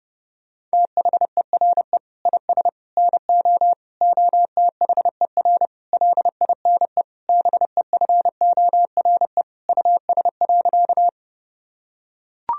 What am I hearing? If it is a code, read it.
Morse code, heard as a keyed tone, about 32 wpm, 700 Hz. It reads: THERE IS NO OTHER LINE BEFORE US.
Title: Sentences from Top 1000 Words - Review Part 1 - 30wpm